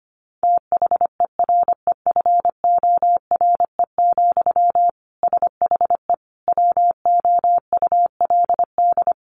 Morse code, heard as a keyed tone, about 25 wpm, 700 Hz. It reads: THEREFORE, SHE WOULD
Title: Sentences from Top 800 Words 25wpm